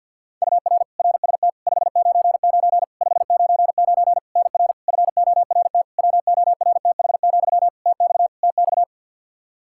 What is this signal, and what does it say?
Morse code, heard as a keyed tone, about 51 wpm, 700 Hz. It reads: UR RST 599 599 NR FORT WORTH, TX TX
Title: Practice QSO - Exchange 1 - Beginning - 50wpm